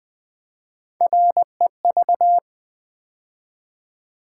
Morse code, heard as a keyed tone, about 20 wpm, 700 Hz. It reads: REV
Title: Build A Word - 3 Letters At A Time - 6-Letter Words - 20wpm